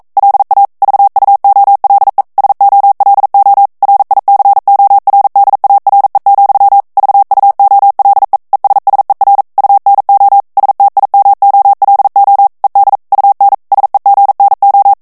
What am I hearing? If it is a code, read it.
Morse code, heard as a keyed tone, about 42 wpm, 800 Hz. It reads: LA VUOLE SOLO RICORDARE, VUOLE ESSER UNO STIMOLO ED UN SEGNO
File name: morse mp3 / ITALIEN texte clair 1 vitesse 40 mots minute